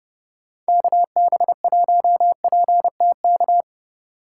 Morse code, heard as a keyed tone, about 30 wpm, 700 Hz. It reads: KB1PTK
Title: Speed Racing - USA 2x3 Call Signs 25wpm